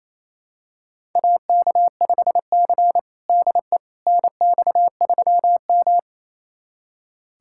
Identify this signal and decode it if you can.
Morse code, heard as a keyed tone, about 28 wpm, 690 Hz. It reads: AK5C DE NX3M
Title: USA Callsign DE Callsign - 4 Characters - 28wpm